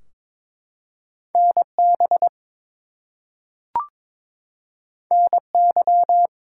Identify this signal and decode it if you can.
Morse code, heard as a keyed tone, about 22 wpm, 700 Hz. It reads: NB NY
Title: US States and Canadian Provinces 22wpm